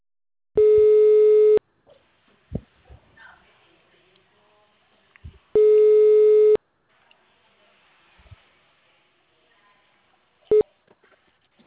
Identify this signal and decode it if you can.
Ringback tone (Europe, 425 Hz).
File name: out- -2027- - - .18362